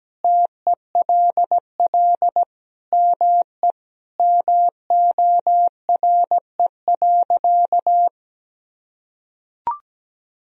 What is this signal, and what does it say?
Morse code, heard as a keyed tone, about 17 wpm, 700 Hz. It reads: TELL ME MORE.
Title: Sentences from Top 1000 Words - Review Part 1 - 17wpm